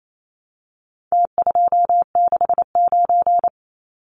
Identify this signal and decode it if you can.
Morse code, heard as a keyed tone, about 28 wpm, 700 Hz. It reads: T269
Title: Numbers - Contest Style - 4-Digit 28wpm